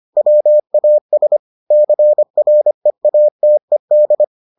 Morse code, heard as a keyed tone, about 25 wpm, 600 Hz. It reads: WAS CREATED